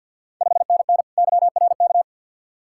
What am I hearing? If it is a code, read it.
Morse code, heard as a keyed tone, about 50 wpm, 690 Hz. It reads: 5NN YRK
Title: Ontario QSO Party Exchange 50wpm